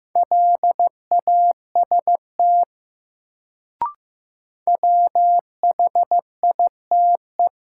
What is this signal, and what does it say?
Morse code, heard as a keyed tone, about 15 wpm, 700 Hz. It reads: LAST WHITE
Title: Sets of 2 Words - New Word Drill - 15wpm